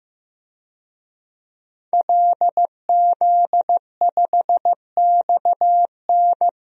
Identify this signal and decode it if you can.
Morse code, heard as a keyed tone, about 15 wpm, 700 Hz. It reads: LZ5XN
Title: DXCC-100 Call Signs - 5 Characters - 15wpm